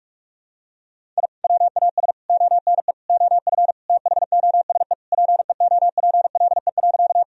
Morse code, heard as a keyed tone, about 46 wpm, 700 Hz. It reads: I WAS ONE OF THOSE PEOPLE.
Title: Sentences from Top 1000 Words - Review Part 1 - 45wpm